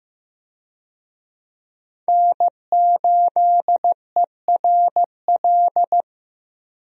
Morse code, heard as a keyed tone, about 15 wpm, 700 Hz. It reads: N8ERL
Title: USA Call Signs - 5 Characters - 15wpm